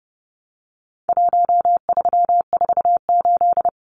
Morse code, heard as a keyed tone, about 30 wpm, 700 Hz. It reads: 1348
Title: Numbers - Contest Style - 4-Digit 30wpm